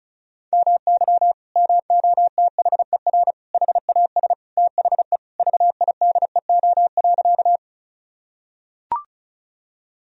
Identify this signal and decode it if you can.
Morse code, heard as a keyed tone, about 36 wpm, 700 Hz. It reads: MY MOTHER HAS THE VIDEO.
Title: Sentences from Top 1000 Words - Review Part 4 - 35wpm